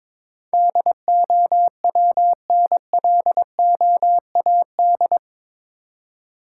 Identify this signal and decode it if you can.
Morse code, heard as a keyed tone, about 22 wpm, 700 Hz. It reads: DOWNLOAD
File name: practice / 22wpm / Eight Letter Words - 22wpm